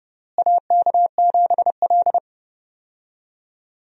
Morse code, heard as a keyed tone, about 30 wpm, 700 Hz. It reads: AK7L
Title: USA Call Signs - 4 Characters - Repeated 3x - 30wpm